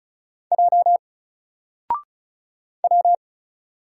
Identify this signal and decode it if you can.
Morse code, heard as a keyed tone, about 35 wpm, 700 Hz. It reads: J W
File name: Single-Letter-Number-35wpm